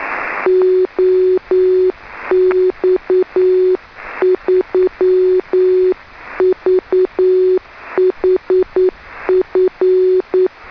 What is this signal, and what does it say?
Morse code, heard as a keyed tone, about 9 wpm, 370 Hz. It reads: OX3VHF
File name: 2m beac 144,418 Mhz